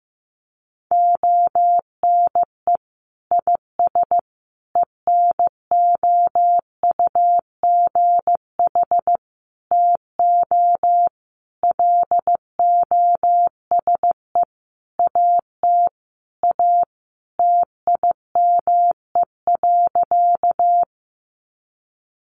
Morse code, heard as a keyed tone, about 15 wpm, 700 Hz. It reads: ONE IS ENOUGH TO LOSE AT A TIME.
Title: Sentences from Top 1100 Words 15wpm